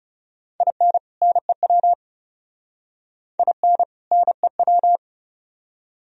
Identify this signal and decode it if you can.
Morse code, heard as a keyed tone, about 40 wpm, 700 Hz. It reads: IN NEW IN NEW